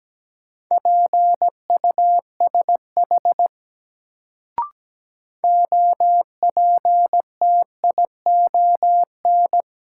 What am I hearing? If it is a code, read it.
Morse code, heard as a keyed tone, about 17 wpm, 700 Hz. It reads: PUSH OPTION
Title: Sentences from Top 1300 Words - New Word Drill - 17wpm